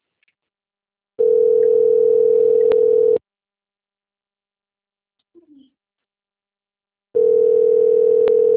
Ringback tone (North America), 440 + 480 Hz.